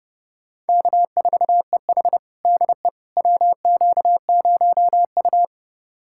Morse code, heard as a keyed tone, about 32 wpm, 700 Hz. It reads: K4EH DE WQ0U
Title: USA Callsign DE Callsign - 4 Characters - 30wpm